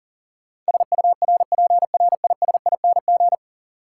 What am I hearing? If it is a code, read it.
Morse code, heard as a keyed tone, about 40 wpm, 700 Hz. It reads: SURPRISING
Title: Sets of 5 Words - New Word Drill - 40wpm